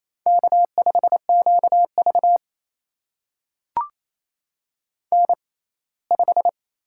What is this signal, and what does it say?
Morse code, heard as a keyed tone, about 28 wpm, spaced as for 19 wpm, 700 Hz. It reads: K5QV N5
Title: USA Call Signs - 6x Character Spacing - 28wpm